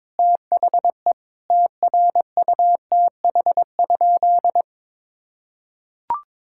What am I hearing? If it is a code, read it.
Morse code, heard as a keyed tone, about 22 wpm, 700 Hz. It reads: THE TRUTH?
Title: Sentences from Top 1000 Words - Review Part 6 - 22wpm